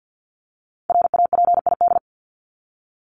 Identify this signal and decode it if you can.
Morse code, heard as a keyed tone, about 50 wpm, 700 Hz. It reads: RAPID
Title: LY Suffix - Top 100 Words with Base Words 50wpm